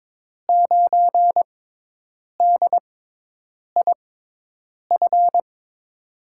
Morse code, heard as a keyed tone, about 22 wpm, spaced as for 8 wpm, 700 Hz. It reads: 9DIF
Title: USA Call Signs - 5 Characters - 6x Character Spacing - 22wpm